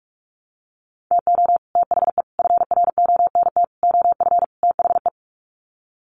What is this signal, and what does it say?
Morse code, heard as a keyed tone, about 44 wpm, 700 Hz. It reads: TO THE FRONT OF THE
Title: Sets of 5 Words 45wpm